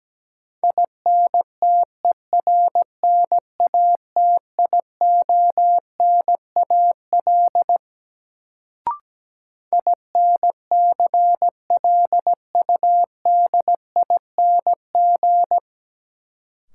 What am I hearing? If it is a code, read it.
Morse code, heard as a keyed tone, about 17 wpm, 700 Hz. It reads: INTERNATIONAL INCLUDING
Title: Sentences from Top 1000 Words - Review Part 4 - New Word Drill - 17wpm